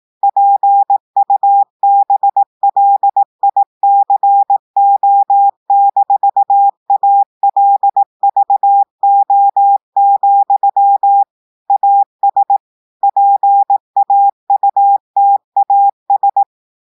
Morse code, heard as a keyed tone, about 18 wpm, 800 Hz. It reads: PUBLICO-ALVO, AS PAUTAS
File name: audios / portugais / portugais texte en clair 2 vitesse 18 mots minutes